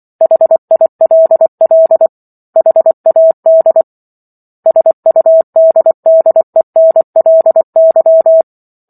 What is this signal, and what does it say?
Morse code, heard as a keyed tone, about 24 wpm, 650 Hz. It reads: HILL HAD SUDDENLY
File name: Book One / 15-Chapter 15 - 24 WPM